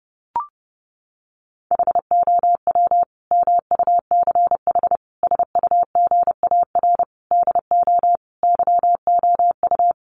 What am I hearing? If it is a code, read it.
Morse code, heard as a keyed tone, about 30 wpm, 700 Hz. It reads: HOW MUCH SUGAR DO YOU
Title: Sentences from Top 1200 Words 30wpm